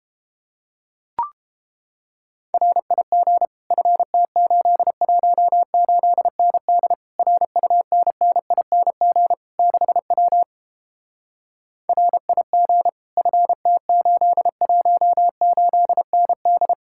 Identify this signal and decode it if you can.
Morse code, heard as a keyed tone, about 29 wpm, 700 Hz. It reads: RIG FT818ND RUNNING 6W RIG FT818ND